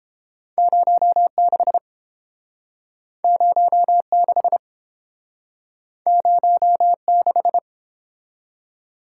Morse code, heard as a keyed tone, about 29 wpm, 700 Hz. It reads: 06 06 06